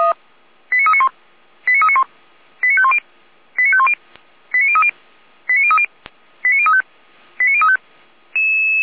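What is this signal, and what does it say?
Five-tone selective call (ZVEI), heard as quick runs of five tones, tones of about 80 ms. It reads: 89391 89391 89510 89510 89030 89030 89035 89035
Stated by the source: Received on an SDR.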